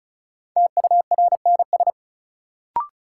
Morse code, heard as a keyed tone, about 36 wpm, 700 Hz. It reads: TURNS
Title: Sets of 5 Words Encore - New Word Drill - 35wpm